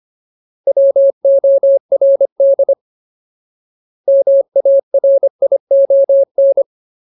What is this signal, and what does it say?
Morse code, heard as a keyed tone, about 25 wpm, 550 Hz. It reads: WORD MARION
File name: Ultimate Word List / 25WPM Ultimate